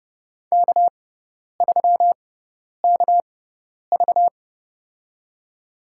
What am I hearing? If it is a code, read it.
Morse code, heard as a keyed tone, about 31 wpm, spaced as for 10 wpm, 700 Hz. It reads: K3KV
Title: USA Call Signs - 4 Characters - 6x Character Spacing - 30wpm